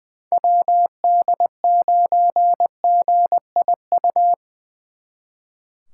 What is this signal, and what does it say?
Morse code, heard as a keyed tone, about 24 wpm, 700 Hz. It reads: WD9GIU
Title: Speed Racing - USA 2x3 Call Signs 20wpm